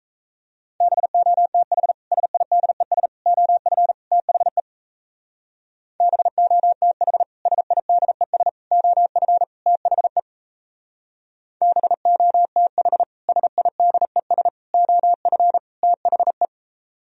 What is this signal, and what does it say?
Morse code, heard as a keyed tone, about 34 wpm, 700 Hz. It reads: BOTH SIDES OF THE BOTH SIDES OF THE BOTH SIDES OF THE
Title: Speed-Racing - Sets of 4 Words 28wpm